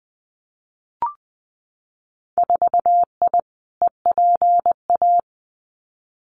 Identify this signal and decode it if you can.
Morse code, heard as a keyed tone, about 19 wpm, 700 Hz. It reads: 4I EPA
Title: Winter Field Day Exchanges 20wpm